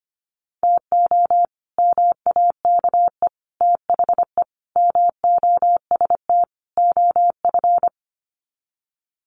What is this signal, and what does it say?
Morse code, heard as a keyed tone, about 25 wpm, 700 Hz. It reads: TO MAKE THE MOST OF
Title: Sets of 5 Words Encore2 25wpm